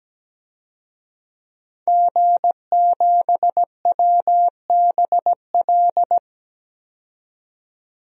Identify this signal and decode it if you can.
Morse code, heard as a keyed tone, about 17 wpm, 700 Hz. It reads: G7WBL
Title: DXCC-100 Call Signs - 5 Characters - 17wpm